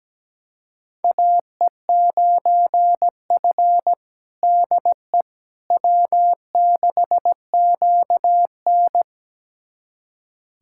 Morse code, heard as a keyed tone, about 17 wpm, 700 Hz. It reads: AE9F DE W6QN
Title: USA Callsign DE Callsign - 4 Characters - 17wpm